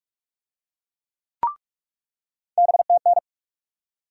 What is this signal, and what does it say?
Morse code, heard as a keyed tone, about 42 wpm, 700 Hz. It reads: BTN
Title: USA Call Sign Postfixes - 3 Letters 45wpm